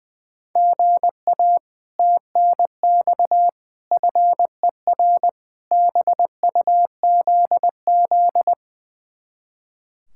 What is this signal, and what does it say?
Morse code, heard as a keyed tone, about 20 wpm, 700 Hz. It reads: GA TNX FER BUZZ
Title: Speed-Racing Common QSO Phrases Encore - 20wpm